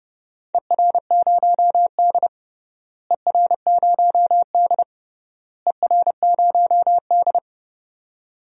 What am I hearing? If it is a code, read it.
Morse code, heard as a keyed tone, about 30 wpm, 700 Hz. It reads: ER0D ER0D ER0D
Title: DXCC-100 Call Signs - 4 Characters - Repeated 3x - 30wpm